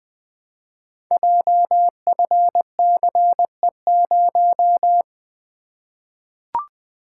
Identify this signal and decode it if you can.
Morse code, heard as a keyed tone, about 21 wpm, 700 Hz. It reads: JFCE0
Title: Code Groups - 5 Letter:Numbers - 20wpm